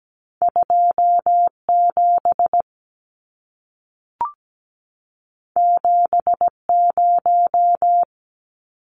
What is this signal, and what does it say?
Morse code, heard as a keyed tone, about 17 wpm, 700 Hz. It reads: 27 70
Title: Numbers - 2-Digit 17wpm